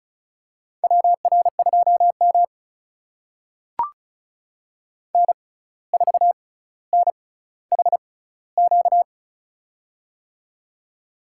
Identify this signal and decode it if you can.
Morse code, heard as a keyed tone, about 34 wpm, spaced as for 17 wpm, 700 Hz. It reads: WR2M N4NHQ
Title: USA Call Signs - 6x Character Spacing - 35wpm